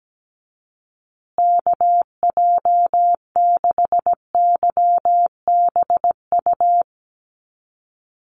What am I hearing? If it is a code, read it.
Morse code, heard as a keyed tone, about 17 wpm, 700 Hz. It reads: KJ6YBU
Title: USA Call Signs Repeated 2x 17wpm